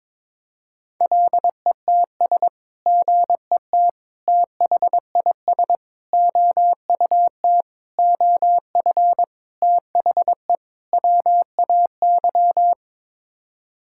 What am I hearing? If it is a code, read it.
Morse code, heard as a keyed tone, about 22 wpm, 700 Hz. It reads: LETS GET THIS OUT OF THE WAY
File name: practice / 22wpm / Sets of 7 Words - 22wpm